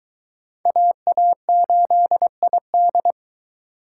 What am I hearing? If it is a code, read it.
Morse code, heard as a keyed tone, about 24 wpm, 700 Hz. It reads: AA8ID